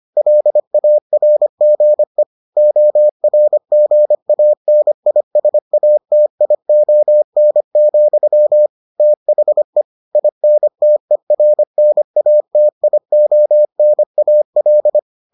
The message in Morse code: LARGE ORGANISATION, THE INTERNATIONAL

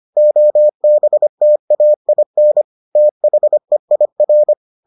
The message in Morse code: OBTAIN THEIR

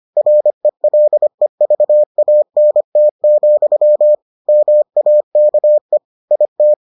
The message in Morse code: RELEVANT, MAKE IT

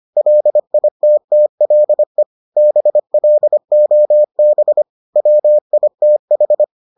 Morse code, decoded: LITTLE BLOB WITH

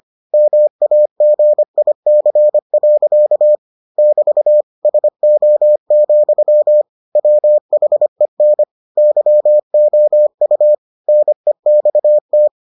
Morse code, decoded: MAGIC. <BT> SO, WHEN YOU NEXT

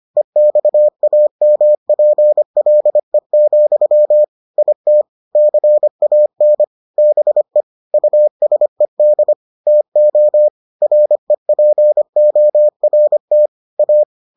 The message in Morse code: EXAMPLE, IT CAN BE USED TO REPORT A

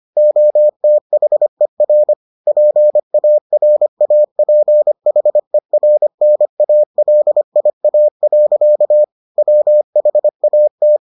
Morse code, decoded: OTHER PARAPHERNALIA. WHAT

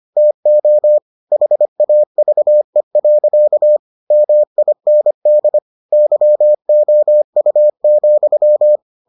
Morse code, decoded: TO HAVE. MIND YOU,

